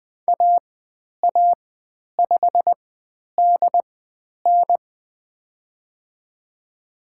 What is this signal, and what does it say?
Morse code, heard as a keyed tone, about 20 wpm, spaced as for 10 wpm, 700 Hz. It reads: AA5DN
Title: 095 Lesson 16g - Letter c - Call Signs 20wpm